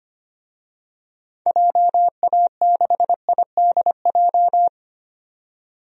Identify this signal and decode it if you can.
Morse code, heard as a keyed tone, about 25 wpm, 700 Hz. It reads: JA6IDJ
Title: DXCC-100 Call Signs - 6 Characters - 25wpm